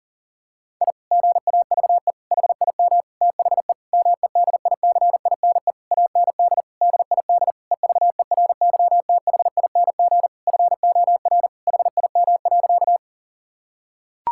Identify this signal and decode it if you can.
Morse code, heard as a keyed tone, about 40 wpm, 700 Hz. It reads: I GAVE HIM THE MEDICINE AND DID EVERYTHING FOR HIM.
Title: Sentences from Top 1000 Words - Review Part 6 - 40wpm